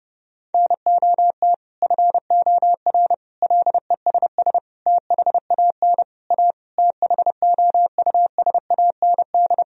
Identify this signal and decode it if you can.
Morse code, heard as a keyed tone, about 32 wpm, 700 Hz. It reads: NOT FOR LESS THAN A THOUSAND
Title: Sentences from Top 1000 Words - Review Part 3 - 30wpm